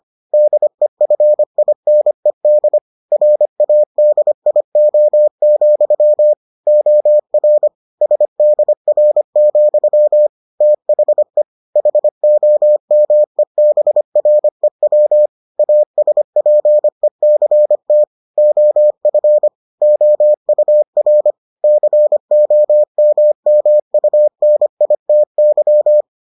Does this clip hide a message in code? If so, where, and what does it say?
Yes: Morse code from 0.3 to 26.0 s, heard as a keyed tone, about 25 wpm, 600 Hz. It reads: DEFINED RADIO, OR SDR, THE HOMEBREW ASPECT OF OUR COMMUNITY